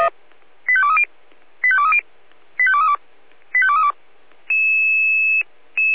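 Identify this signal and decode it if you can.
Five-tone selective call (ZVEI), heard as quick runs of five tones, tones of about 70 ms. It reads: 85230 85230 85232 85232
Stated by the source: ZVEI five-tone selective call received by SDR.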